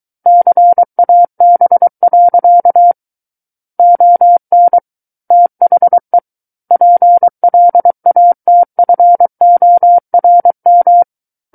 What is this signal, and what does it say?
Morse code, heard as a keyed tone, about 23 wpm, 700 Hz. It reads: CAB. ON THE PLATFORM
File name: Book One / 14-Chapter 14 - 23 WPM